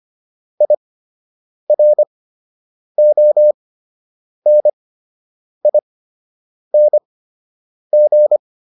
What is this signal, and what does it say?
Morse code, heard as a keyed tone, about 25 wpm, spaced as for 8 wpm, 600 Hz. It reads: IRONING